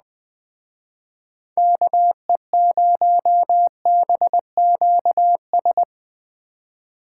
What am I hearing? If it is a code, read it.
Morse code, heard as a keyed tone, about 20 wpm, 700 Hz. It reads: KE0BQS
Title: Speed Racing - USA 2x3 Call Signs 17wpm